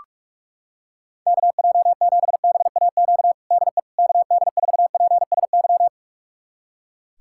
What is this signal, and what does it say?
Morse code, heard as a keyed tone, about 41 wpm, 690 Hz. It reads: KJ7BAQ DE KD4PSY
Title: USA Callsign DE Callsign 45wpm